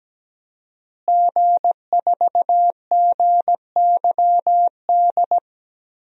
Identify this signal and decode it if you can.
Morse code, heard as a keyed tone, about 17 wpm, 700 Hz. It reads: G4GYD
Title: DXCC-100 Call Signs - 5 Characters - Repeated 2x - 17wpm